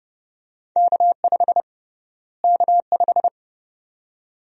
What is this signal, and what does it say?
Morse code, heard as a keyed tone, about 30 wpm, 700 Hz. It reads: K5 K5
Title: USA Call Sign Prefixes - Repeated 2x - 30wpm